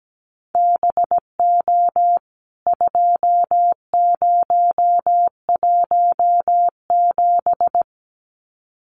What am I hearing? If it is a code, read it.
Morse code, heard as a keyed tone, about 17 wpm, 700 Hz. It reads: BO 2017
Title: CWT Exchange 17wpm